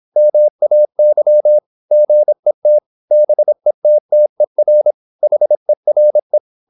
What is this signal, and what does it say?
Morse code, heard as a keyed tone, about 26 wpm, 600 Hz. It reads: MAY GET BETTER HERE
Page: Never 26wpm | CW med Gnister